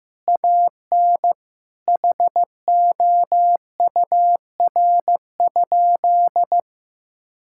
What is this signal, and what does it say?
Morse code, heard as a keyed tone, about 15 wpm, 700 Hz. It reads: AN HOUR?
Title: Sentences from Top 1000 Words - Review Part 5 - 15wpm